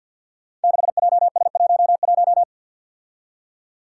Morse code, heard as a keyed tone, about 48 wpm, 700 Hz. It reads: 6JS11
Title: Five Challenging Letter-Number Combinations 50wpm